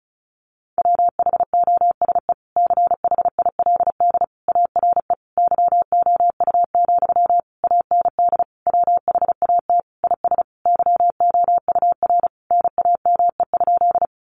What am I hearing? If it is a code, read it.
Morse code, heard as a keyed tone, about 34 wpm, 700 Hz. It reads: WHOSE CHILD ARE YOU, AND WHAT IS YOUR NAME?
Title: Sentences from Top 1200 Words 35wpm